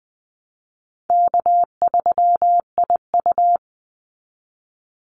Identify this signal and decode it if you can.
Morse code, heard as a keyed tone, about 20 wpm, 700 Hz. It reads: K3IU
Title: Easy Call Signs Repeated 3x - 20wpm